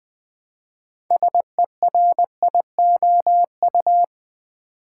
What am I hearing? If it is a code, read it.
Morse code, heard as a keyed tone, about 20 wpm, 700 Hz. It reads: SERIOU
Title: Build A Word - 3 Letters At A Time - 9-Letter Words - 20wpm